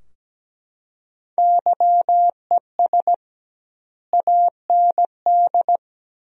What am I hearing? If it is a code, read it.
Morse code, heard as a keyed tone, about 17 wpm, 700 Hz. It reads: YES AND
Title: 2x Word-Spacing - Sentences from Top 500 Words - 17wpm